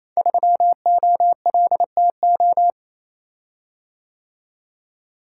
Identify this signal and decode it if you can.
Morse code, heard as a keyed tone, about 27 wpm, 700 Hz. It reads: 3OLTO
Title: Code Groups - 5 Letter:Numbers - 28wpm